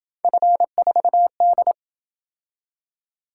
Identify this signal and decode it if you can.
Morse code, heard as a keyed tone, about 27 wpm, 700 Hz. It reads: F4D